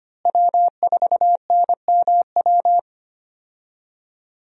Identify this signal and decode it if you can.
Morse code, heard as a keyed tone, about 25 wpm, 700 Hz. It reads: W4NMW
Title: USA Call Signs - 5 Characters - Repeated 3x - 25wpm